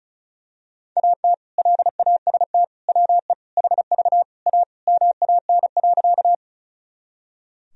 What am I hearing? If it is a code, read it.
Morse code, heard as a keyed tone, about 36 wpm, 700 Hz. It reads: AT LAST WE HV A MAN.
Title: Sentences from Top 300 Words and Abbreviations 35wpm